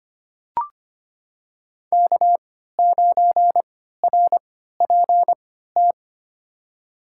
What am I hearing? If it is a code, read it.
Morse code, heard as a keyed tone, about 25 wpm, spaced as for 14 wpm, 700 Hz. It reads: K9RPT
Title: USA Call Signs - 5 Characters - 3x Character Spacing - 25wpm